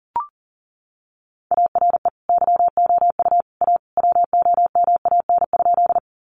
Morse code, heard as a keyed tone, about 40 wpm, 700 Hz. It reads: ARE YOU A WOMAN?
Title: Sentences from Top 1100 Words 40wpm